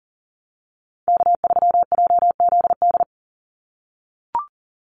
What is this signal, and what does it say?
Morse code, heard as a keyed tone, about 39 wpm, 700 Hz. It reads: K3JZD
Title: NA SOTA Chaser Callsigns 40wpm